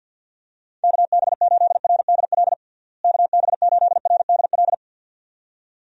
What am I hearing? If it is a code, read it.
Morse code, heard as a keyed tone, about 52 wpm, 700 Hz. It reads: KB8RDL KB8RDL
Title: USA Call Signs - 6 Characters - Repeated 2x - 50wpm